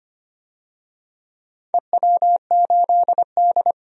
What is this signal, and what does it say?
Morse code, heard as a keyed tone, about 25 wpm, 700 Hz. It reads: EW8D
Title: DXCC-100 Call Signs - 4 Characters - 25wpm